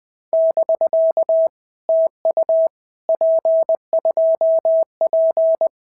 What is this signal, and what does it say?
Morse code, heard as a keyed tone, about 20 wpm, 650 Hz. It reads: <BK> TU P2P
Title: POTA QSO Practice - 20wpm